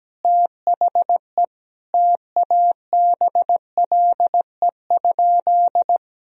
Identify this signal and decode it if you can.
Morse code, heard as a keyed tone, about 17 wpm, 700 Hz. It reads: THE TABLE?
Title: Sentences from Top 1000 Words - Review Part 2 - 17wpm